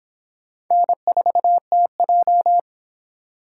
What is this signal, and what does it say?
Morse code, heard as a keyed tone, about 30 wpm, 700 Hz. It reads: N4TJ